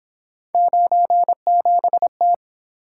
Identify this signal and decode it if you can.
Morse code, heard as a keyed tone, about 26 wpm, 700 Hz. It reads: 97T
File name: Speed-Racing - Three Letter-Number Combinations - 22wpm